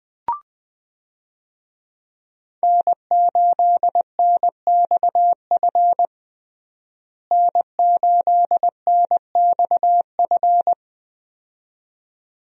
Morse code, heard as a keyed tone, about 20 wpm, 700 Hz. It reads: N8NXF N8NXF
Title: USA Call Signs - 5 Characters - Repeated 2x - 20wpm